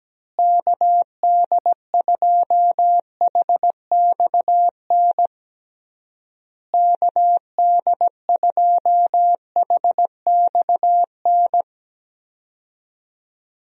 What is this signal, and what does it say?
Morse code, heard as a keyed tone, about 17 wpm, 700 Hz. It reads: KD2HXN KD2HXN
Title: USA Call Signs - 6 Characters - Repeated 3x - 17wpm